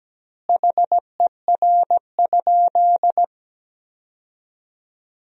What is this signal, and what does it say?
Morse code, heard as a keyed tone, about 17 wpm, 700 Hz. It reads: HER?
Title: 2x Word-Spacing - Sentences from Top 100 Words - 17wpm